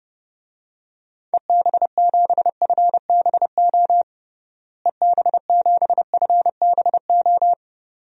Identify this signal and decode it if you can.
Morse code, heard as a keyed tone, about 29 wpm, 700 Hz. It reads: EB7FBO EB7FBO
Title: DXCC-100 Call Signs - 6 Characters - Repeated 2x - 30wpm